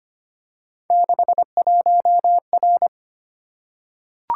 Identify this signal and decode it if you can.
Morse code, heard as a keyed tone, about 25 wpm, 700 Hz. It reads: 61R
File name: Three Letter-Number Combinations - 25wpm